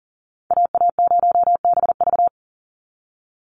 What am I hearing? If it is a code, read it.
Morse code, heard as a keyed tone, about 40 wpm, 700 Hz. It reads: AA0BV
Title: NA SOTA Activator Callsigns 40wpm